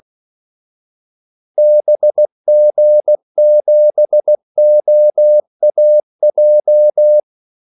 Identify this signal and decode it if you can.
Morse code, heard as a keyed tone, about 16 wpm, 600 Hz. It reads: BG7OAJ
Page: Kaldesignaler 16 wpm